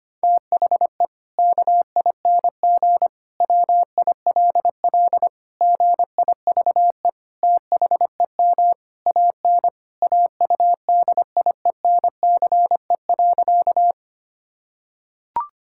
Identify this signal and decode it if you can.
Morse code, heard as a keyed tone, about 25 wpm, 700 Hz. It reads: THE KING WILL GIVE THEM AN AUDIENCE.
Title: Sentences from Top 1000 Words - Review Part 5 - 25wpm